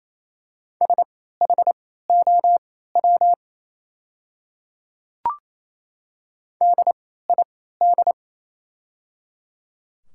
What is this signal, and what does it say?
Morse code, heard as a keyed tone, about 28 wpm, spaced as for 16 wpm, 700 Hz. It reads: SHOW DID
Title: 3x Character Spacing - Top 300 Words - 28wpm